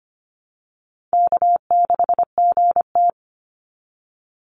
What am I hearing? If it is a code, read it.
Morse code, heard as a keyed tone, about 25 wpm, 700 Hz. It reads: K6GT
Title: Easy Call Signs - 25wpm